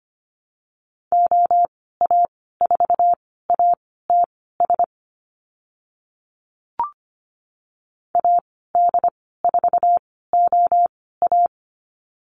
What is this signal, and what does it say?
Morse code, heard as a keyed tone, about 25 wpm, spaced as for 16 wpm, 700 Hz. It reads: OA4ATS AD4OA
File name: Lesson 15g - Number 5 - Call Signs 25wpm